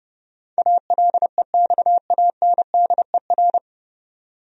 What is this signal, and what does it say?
Morse code, heard as a keyed tone, about 30 wpm, 700 Hz. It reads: ALEXANDER
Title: USA Common Names - 30wpm